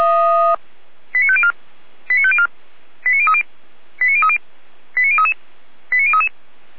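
Five-tone selective call (ZVEI), heard as quick runs of five tones, tones of about 70 ms. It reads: 89504 89504 89030 89030 89033 89033
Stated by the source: ZVEI five-tone selective call received on an SDR.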